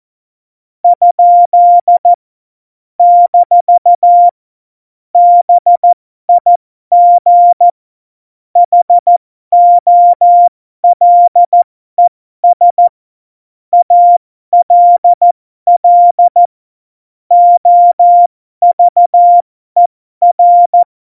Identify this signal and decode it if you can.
Morse code, heard as a keyed tone, about 14 wpm, spaced as for 12 wpm, 700 Hz. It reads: ? - BIG HOLES ALL OVER
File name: Morsekurs / Witze